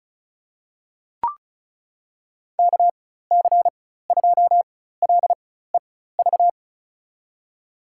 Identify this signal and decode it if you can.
Morse code, heard as a keyed tone, about 36 wpm, spaced as for 16 wpm, 700 Hz. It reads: KC2LEV
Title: USA Call Signs - 6 Characters - 4x Character Spacing - 35wpm